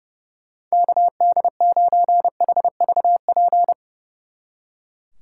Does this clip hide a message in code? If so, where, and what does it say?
Yes: Morse code from 0.7 to 3.7 s, heard as a keyed tone, about 38 wpm, 700 Hz. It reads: KD9HVP